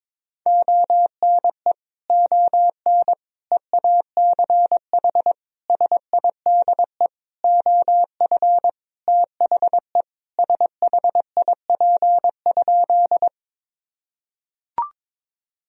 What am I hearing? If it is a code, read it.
Morse code, heard as a keyed tone, about 22 wpm, 700 Hz. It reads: ONE ON EACH SIDE OF THE SHIP?
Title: Sentences from Top 1000 Words - Review Part 2 - 22wpm